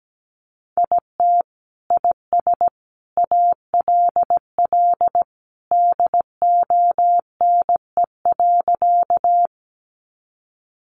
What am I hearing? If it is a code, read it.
Morse code, heard as a keyed tone, about 17 wpm, 700 Hz. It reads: IT IS ALL DONE.
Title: Sentences from Top 500 Words 17wpm